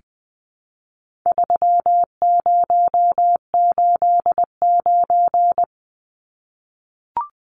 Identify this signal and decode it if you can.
Morse code, heard as a keyed tone, about 20 wpm, 700 Hz. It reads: 3089
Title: Numbers - 4-Digit 20wpm